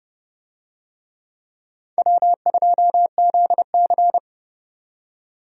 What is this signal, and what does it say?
Morse code, heard as a keyed tone, about 32 wpm, 700 Hz. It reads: W2ZC
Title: USA Call Signs - 4 Characters - 30wpm